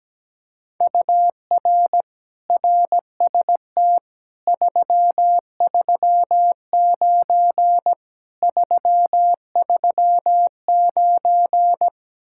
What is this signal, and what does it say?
Morse code, heard as a keyed tone, about 17 wpm, 700 Hz. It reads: UR RST 339 339
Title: Speed-Racing Common QSO Phrases Encore - 17wpm